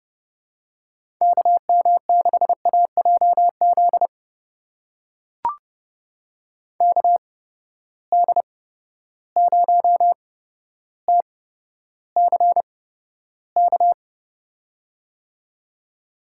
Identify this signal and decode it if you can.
Morse code, heard as a keyed tone, about 30 wpm, spaced as for 13 wpm, 700 Hz. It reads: KM6AJZ KD0TCK
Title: USA Call Signs - 6 Characters - 8x Character Spacing - 30wpm